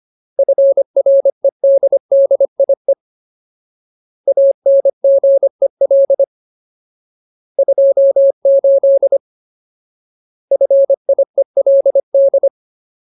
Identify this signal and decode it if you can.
Morse code, heard as a keyed tone, about 25 wpm, 550 Hz. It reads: FREDDIE ANGEL 28 FIELD
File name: Ultimate Word List / 25WPM Ultimate